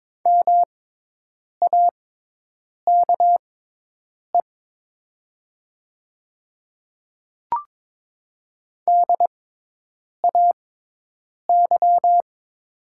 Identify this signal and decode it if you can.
Morse code, heard as a keyed tone, about 22 wpm, spaced as for 8 wpm, 700 Hz. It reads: MAKE DAY
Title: 6x Character Spacing - Top 200 Words - 22wpm